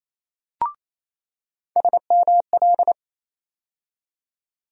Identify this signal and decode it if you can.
Morse code, heard as a keyed tone, about 28 wpm, 700 Hz. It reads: SML
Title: USA Call Sign Postfixes - 3 Letters 28wpm